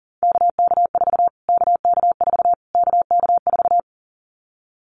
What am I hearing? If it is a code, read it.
Morse code, heard as a keyed tone, about 40 wpm, 700 Hz. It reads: KK4 KK4 KK4
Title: Callsign Prefixes for Top 25 DXCC - Repeated 3x - 40wpm